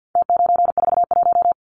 Morse code, heard as a keyed tone, about 50 wpm, 700 Hz. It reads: T941
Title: Numbers - Contest Style - 4-Digit 50wpm